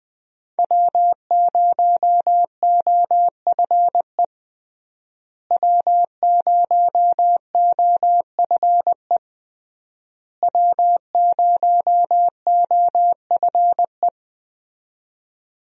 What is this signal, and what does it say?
Morse code, heard as a keyed tone, about 21 wpm, 700 Hz. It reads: W0OFE W0OFE W0OFE
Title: USA Call Signs - 5 Characters - Repeated 3x - 20wpm